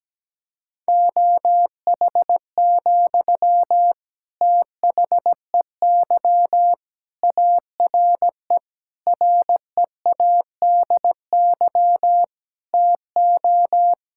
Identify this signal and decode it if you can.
Morse code, heard as a keyed tone, about 17 wpm, 700 Hz. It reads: OH, THEY ARE READY TO
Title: Sentences from Top 1600 Words 17wpm